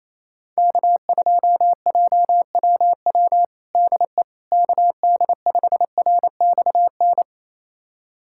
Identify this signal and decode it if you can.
Morse code, heard as a keyed tone, about 28 wpm, 700 Hz. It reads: K2JWW DE KD5RXN
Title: USA Callsign DE Callsign 28wpm